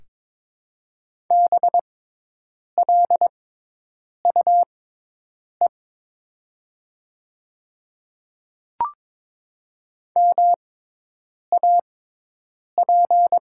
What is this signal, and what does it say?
Morse code, heard as a keyed tone, about 21 wpm, spaced as for 8 wpm, 700 Hz. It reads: BLUE MAP
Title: 6x Character Spacing - Top 500 Words - 22wpm